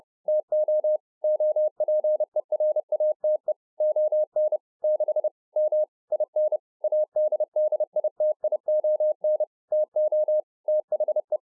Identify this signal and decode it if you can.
Morse code, heard as a keyed tone, about 30 wpm, 610 Hz. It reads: TO OPERATE ON 6 M IN ADDITION TO THE